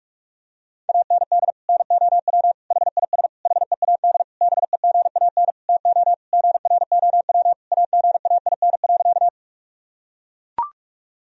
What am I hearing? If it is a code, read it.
Morse code, heard as a keyed tone, about 41 wpm, 700 Hz. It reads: AND NOW HIS HEAD BEGAN TO GROW AGAIN.
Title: Sentences from Top 1000 Words - Review Part 1 - 45wpm